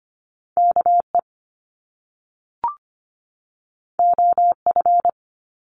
Morse code, heard as a keyed tone, about 25 wpm, 700 Hz. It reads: KE OF
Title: Top Two Letter Combinations in Words 25wpm